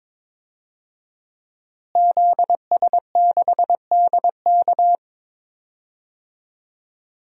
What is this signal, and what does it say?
Morse code, heard as a keyed tone, about 22 wpm, 700 Hz. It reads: ZS6DK
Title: DXCC-100 Call Signs - 5 Characters - Repeated 2x - 22wpm